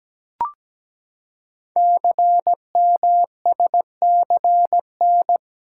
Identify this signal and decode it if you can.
Morse code, heard as a keyed tone, about 17 wpm, 700 Hz. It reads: CMSCN
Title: Code Groups - 5 Letters - 17wpm